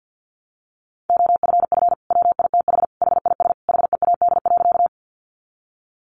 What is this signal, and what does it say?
Morse code, heard as a keyed tone, about 50 wpm, 700 Hz. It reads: OFF WITH HIS HEAD.
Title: Sentences from Top 300 Words 50wpm